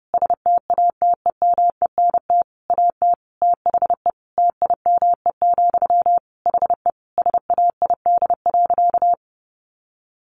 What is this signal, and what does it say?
Morse code, heard as a keyed tone, about 30 wpm, 700 Hz. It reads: STATEMENT AT THE TIME, HE SAID.
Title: Sentences from Top 1200 Words 30wpm